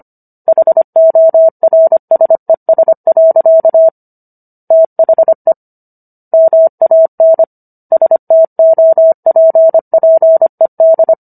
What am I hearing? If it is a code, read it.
Morse code, heard as a keyed tone, about 25 wpm, 650 Hz. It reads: HORSES. THE MAN STOPPED